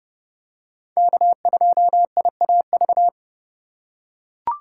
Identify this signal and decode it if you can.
Morse code, heard as a keyed tone, about 30 wpm, 700 Hz. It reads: K2IAV
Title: USA Call Signs - 5 Characters - 2x Character Spacing - 30wpm